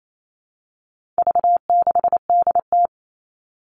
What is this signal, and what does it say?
Morse code, heard as a keyed tone, about 28 wpm, 700 Hz. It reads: V6DT
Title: All Cut Numbers - 4-Digit 28wpm